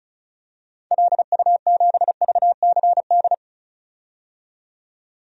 Morse code, heard as a keyed tone, about 36 wpm, 700 Hz. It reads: LU7VCD
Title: DXCC-100 Call Signs - 6 Characters - 35wpm